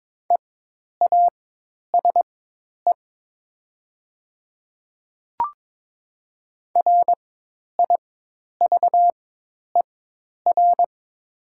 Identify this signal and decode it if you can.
Morse code, heard as a keyed tone, about 22 wpm, spaced as for 10 wpm, 700 Hz. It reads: EASE RIVER
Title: 4x Character Spacing - Top 300 Words - 22wpm